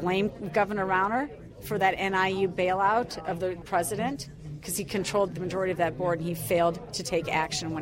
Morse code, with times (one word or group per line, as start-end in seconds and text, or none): none